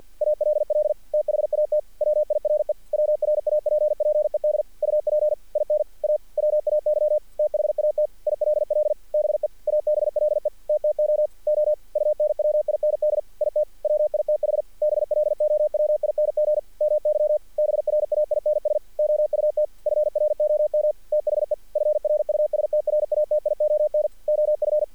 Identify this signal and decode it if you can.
Morse code, heard as a keyed tone, about 48 wpm, 610 Hz. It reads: ALL THAT WIRE WRAPPED UP IN A WAY THAT ILL BE ABLE TTOO UNWIND IT WITH BLOWING MY BRAINS OUT FROM THE FRUSTRATION OF
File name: archived / CRSnet